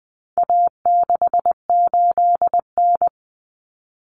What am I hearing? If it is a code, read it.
Morse code, heard as a keyed tone, about 20 wpm, 700 Hz. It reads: A68N
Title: Common Cut Numbers - 4-Digit 20wpm